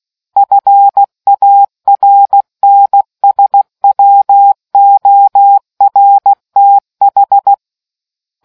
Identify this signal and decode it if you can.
Morse code, heard as a keyed tone, about 16 wpm, 800 Hz. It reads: FARNSWORTH